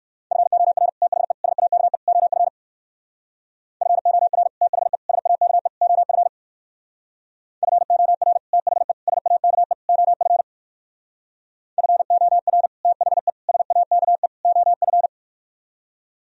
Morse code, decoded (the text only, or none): FOR THE SAKE OF FOR THE SAKE OF FOR THE SAKE OF FOR THE SAKE OF